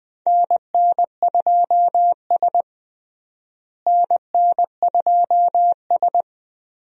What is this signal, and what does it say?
Morse code, heard as a keyed tone, about 21 wpm, 700 Hz. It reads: NN2S NN2S
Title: USA Call Signs - 4 Characters - Repeated 3x - 20wpm